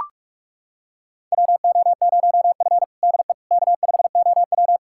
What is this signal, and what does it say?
Morse code, heard as a keyed tone, about 43 wpm, 700 Hz. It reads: WO0F DE K5OW
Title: USA Callsign DE Callsign - 4 Characters - 45wpm